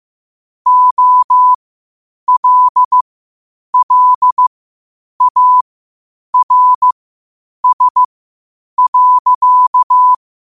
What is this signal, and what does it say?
Morse code, heard as a keyed tone, about 15 wpm, spaced as for 8 wpm, 1000 Hz. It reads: OLLARS.
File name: qotdfiles / 8wpm